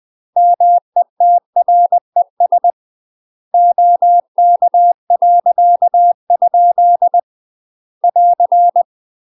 Morse code, heard as a keyed tone, about 20 wpm, 700 Hz. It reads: METRES OK.? <AR>
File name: UK MASTER QSOs TRAINING / 20WPM / UK MASTER QSOs 0036